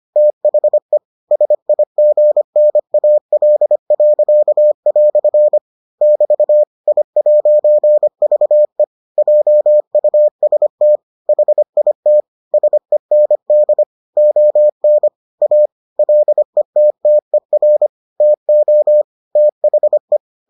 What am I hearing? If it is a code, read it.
Morse code, heard as a keyed tone, about 25 wpm, 600 Hz. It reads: THE SIGNAL." <BT> I'VE JUST HIT SEND ON A LETTER TO THE